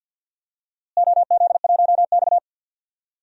Morse code, heard as a keyed tone, about 52 wpm, 700 Hz. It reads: OZ1X
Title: DXCC-100 Call Signs - 4 Characters - Repeated 2x - 50wpm